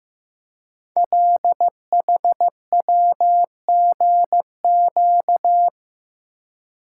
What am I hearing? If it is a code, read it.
Morse code, heard as a keyed tone, about 15 wpm, 700 Hz. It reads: LHWGQ
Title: Code Groups - 5 Letters - 15wpm